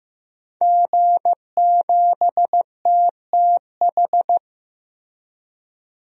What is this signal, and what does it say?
Morse code, heard as a keyed tone, about 16 wpm, 700 Hz. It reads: G7TTH
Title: DXCC-100 Call Signs - 5 Characters - Repeated 3x - 15wpm